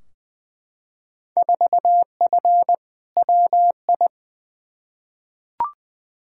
Morse code, heard as a keyed tone, about 21 wpm, 700 Hz. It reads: 4F WI
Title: Field Day Exchanges - 20wpm